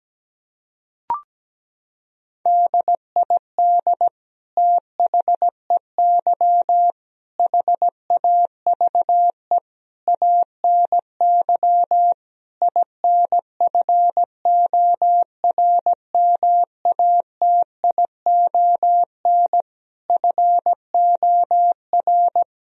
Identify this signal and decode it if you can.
Morse code, heard as a keyed tone, about 17 wpm, 700 Hz. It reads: DID THEY HAVE ANY INFORMATION FOR
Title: Sentences from Top 1000 Words - Review Part 3 - 17wpm